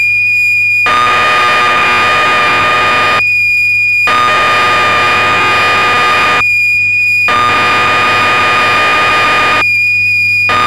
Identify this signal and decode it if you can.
Cassette data (Kansas City type).